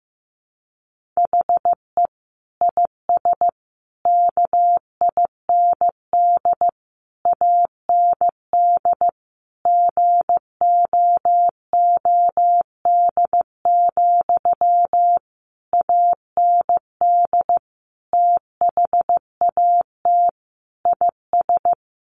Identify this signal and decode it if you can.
Morse code, heard as a keyed tone, about 14 wpm, 700 Hz. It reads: HE IS KIND AND GOOD, AND THAT IS
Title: Sentences from Top 1400 Words 15wpm